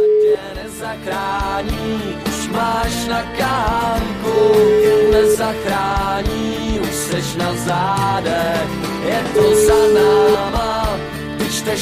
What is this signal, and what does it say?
Ringback tone (Europe, 425 Hz), with noise.